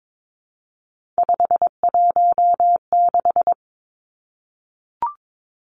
Morse code, heard as a keyed tone, about 22 wpm, 700 Hz. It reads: 516
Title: Three Challenging Letter-Number Combinations 22wpm